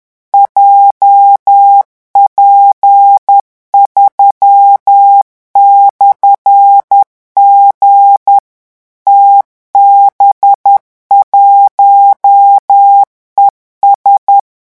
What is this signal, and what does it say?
Morse code, heard as a keyed tone, about 10 wpm, 800 Hz. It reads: JP3/G TB1ES
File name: morse mp3 / revision13-vitesse 10 mots